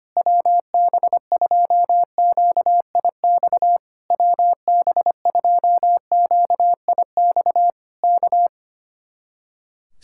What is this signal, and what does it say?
Morse code, heard as a keyed tone, about 25 wpm, 700 Hz. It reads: WB2QIX WB2QIX K
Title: Speed-Racing Common QSO Phrases - 25wpm